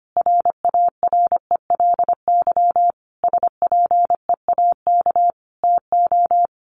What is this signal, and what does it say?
Morse code, heard as a keyed tone, about 25 wpm, 700 Hz. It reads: RARELY SPEAK TO
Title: Sentences from Top 1400 Words 25wpm